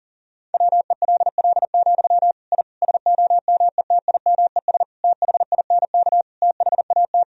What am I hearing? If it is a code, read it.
Morse code, heard as a keyed tone, about 40 wpm, 700 Hz. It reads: WELL, I SOMETIMES THINK THAT
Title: Sentences from Top 1000 Words - Review Part 3 - 40wpm